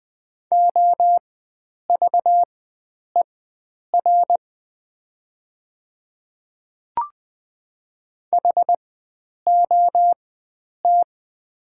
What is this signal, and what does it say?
Morse code, heard as a keyed tone, about 20 wpm, spaced as for 9 wpm, 700 Hz. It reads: OVER HOT
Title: 4x Character Spacing - Top 100 Words - 20wpm